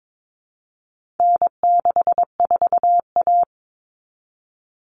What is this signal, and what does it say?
Morse code, heard as a keyed tone, about 22 wpm, 700 Hz. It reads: N64A
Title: Common Cut Numbers - 4-Digit 22wpm